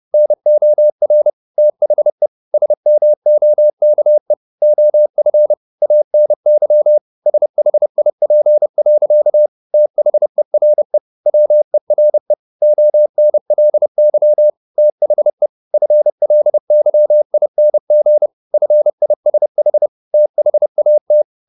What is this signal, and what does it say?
Morse code, heard as a keyed tone, about 30 wpm, 600 Hz. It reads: NOR THE SMOKE OF ANY SHIP. THERE WERE ONLY THE FLYING FISH THAT